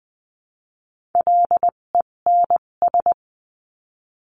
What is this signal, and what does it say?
Morse code, heard as a keyed tone, about 20 wpm, spaced as for 17 wpm, 700 Hz. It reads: LENS
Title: 257 Lesson 42b - Words 14-20wpm